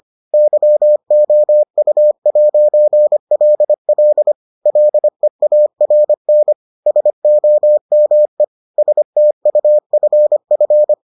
Morse code, heard as a keyed tone, about 25 wpm, 600 Hz. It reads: YOU'LL LEARN SOME STUFF